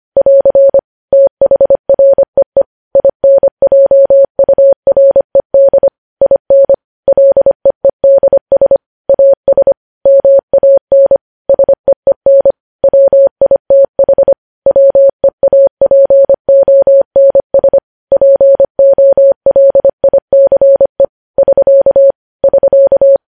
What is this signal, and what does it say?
Morse code, heard as a keyed tone, about 25 wpm, 550 Hz. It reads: <AR> THREE INJURED IN LEEDS AS MAN SEEN WITH WEAPONS POLICE <SK> <SK>